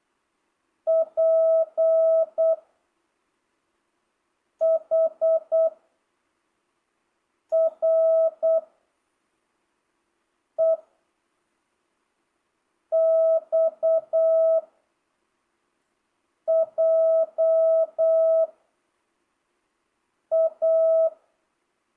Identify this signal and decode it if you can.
Morse code, heard as a keyed tone, about 8 wpm, 660 Hz. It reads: P H R E X J A
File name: jh8cbh / 2025cw